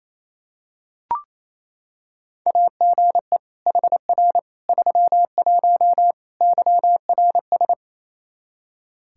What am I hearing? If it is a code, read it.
Morse code, heard as a keyed tone, about 28 wpm, 700 Hz. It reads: AGE HR 31 YRS
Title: Practice QSO - Exchange 3 - More Info - 28wpm